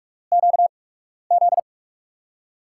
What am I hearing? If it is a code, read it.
Morse code, heard as a keyed tone, about 44 wpm, spaced as for 12 wpm, 700 Hz. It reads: QZ
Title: USA Call Signs - 8x Character Spacing - 45wpm